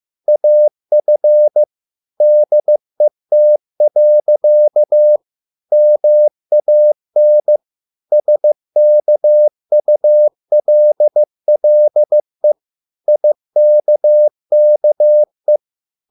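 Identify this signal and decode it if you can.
Morse code, heard as a keyed tone, about 15 wpm, 600 Hz. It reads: AF DET. MAN SKULLE IKKE
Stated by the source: Morse code at a slow 15 wpm from CW med Gnister.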